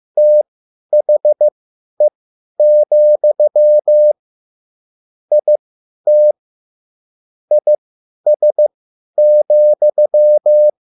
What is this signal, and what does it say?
Morse code, heard as a keyed tone, about 15 wpm, spaced as for 10 wpm, 600 Hz. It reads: THE, IT IS,
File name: QSL / CW lesson